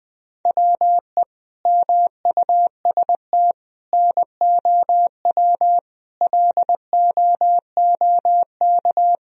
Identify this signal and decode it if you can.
Morse code, heard as a keyed tone, about 21 wpm, 700 Hz. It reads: WE MUST NOW LOOK
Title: Sentences from Top 1000 Words - Review Part 4 - 20wpm